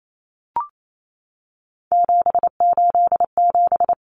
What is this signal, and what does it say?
Morse code, heard as a keyed tone, about 28 wpm, 700 Hz. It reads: 787
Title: Common Cut Numbers - 3-Digit 28wpm